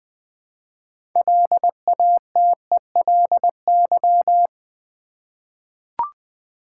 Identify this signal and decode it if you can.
Morse code, heard as a keyed tone, about 20 wpm, 700 Hz. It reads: LATELY
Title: 6 Letter Words - First 3 Letters Given - 20wpm